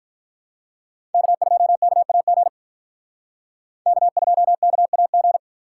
Morse code, heard as a keyed tone, about 46 wpm, 700 Hz. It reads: K2KAG K2KAG